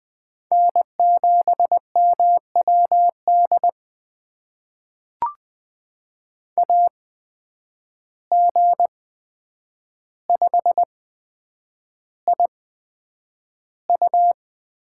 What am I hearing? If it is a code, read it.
Morse code, heard as a keyed tone, about 20 wpm, spaced as for 9 wpm, 700 Hz. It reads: N7MWD AG5IU
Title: USA Call Signs - 5 Characters - 8x Character Spacing - 20wpm